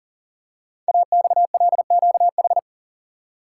Morse code, heard as a keyed tone, about 44 wpm, 700 Hz. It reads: AXLQH
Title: Code Groups - 5 Letters - 40wpm